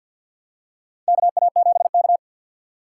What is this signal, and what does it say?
Morse code, heard as a keyed tone, about 53 wpm, 700 Hz. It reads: KA7K
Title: USA Call Signs - 6x Character Spacing - 50wpm